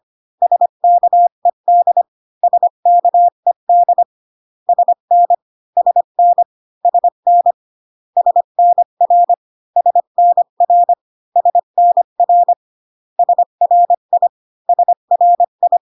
Morse code, heard as a keyed tone, about 25 wpm, 700 Hz. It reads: SKED SKED SN SN SN SNR SNR SNR SRI SRI